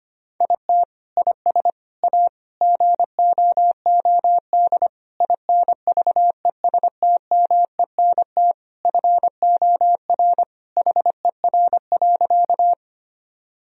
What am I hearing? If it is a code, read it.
Morse code, heard as a keyed tone, about 25 wpm, 700 Hz. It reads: IT IS A GOOD INVESTMENT FOR HER.
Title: Sentences from Top 1000 Words - Review Part 4 - 25wpm